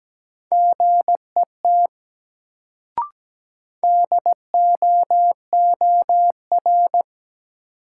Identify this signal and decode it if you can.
Morse code, heard as a keyed tone, about 17 wpm, 700 Hz. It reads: GET DOOR
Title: Sets of 2 Words - New Word Drill - 17wpm